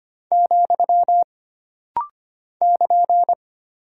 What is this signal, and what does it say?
Morse code, heard as a keyed tone, about 25 wpm, 700 Hz. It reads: , <KN>
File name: Single Punctuation-Prosign - Rapid-Fire - 25wpm